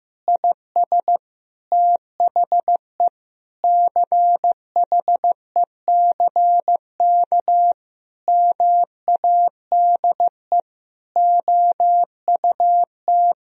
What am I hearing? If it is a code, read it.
Morse code, heard as a keyed tone, about 16 wpm, 700 Hz. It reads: IS THE CHECK MADE OUT
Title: Sentences from Top 1000 Words - Review Part 3 - 15wpm